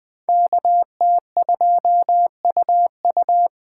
Morse code, heard as a keyed tone, about 24 wpm, 700 Hz. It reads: KT2UU
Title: Speed Racing - USA Call Signs 20wpm